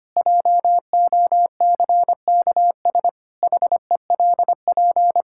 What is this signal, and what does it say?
Morse code, heard as a keyed tone, about 25 wpm, 700 Hz. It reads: JOCKS HELP
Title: Pangram Challenge v2 - 25wpm